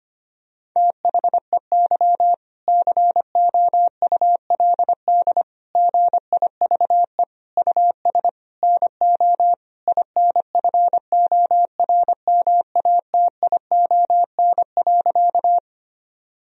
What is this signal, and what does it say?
Morse code, heard as a keyed tone, about 25 wpm, 700 Hz. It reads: THEY COULD GIVE US NO INFORMATION.
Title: Sentences from Top 1000 Words - Review Part 3 - 25wpm